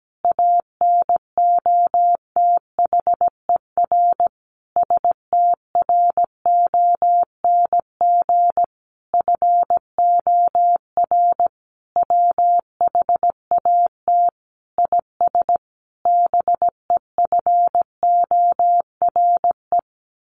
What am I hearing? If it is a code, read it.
Morse code, heard as a keyed tone, about 17 wpm, 700 Hz. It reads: ANOTHER STRONG FOR WHAT IS BEFORE
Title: Sentences from Top 600 Words 17wpm